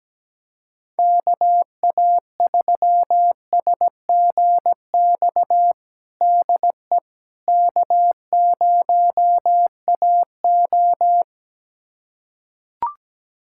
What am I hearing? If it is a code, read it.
Morse code, heard as a keyed tone, about 17 wpm, 700 Hz. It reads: KA3SGX DE K0AO
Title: USA Callsign DE Callsign 17wpm